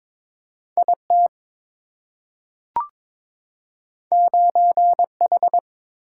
Morse code, heard as a keyed tone, about 22 wpm, 700 Hz. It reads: IT 9H
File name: practice / 22wpm / Two Letter-Number Combinations - 22wpm